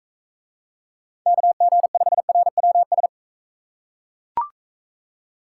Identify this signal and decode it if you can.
Morse code, heard as a keyed tone, about 40 wpm, 700 Hz. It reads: KG5RWS